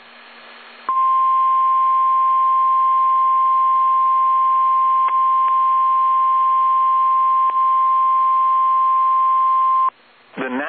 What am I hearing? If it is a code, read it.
Emergency Alert System (SAME) tones (the 1050 Hz attention tone), with speech. It is heard from 0.9 to 10.0 s.
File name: eas / KSDZKDJL / KSDZKDJL